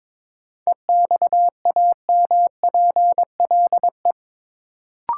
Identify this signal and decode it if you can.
Morse code, heard as a keyed tone, about 22 wpm, 700 Hz. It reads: EXAMPLE
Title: Sets of 2 Words - New Word Drill - 22wpm